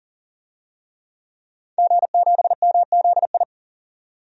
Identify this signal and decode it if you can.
Morse code, heard as a keyed tone, about 40 wpm, 700 Hz. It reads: G7MZI
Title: DXCC-100 Call Signs - 5 Characters - 40wpm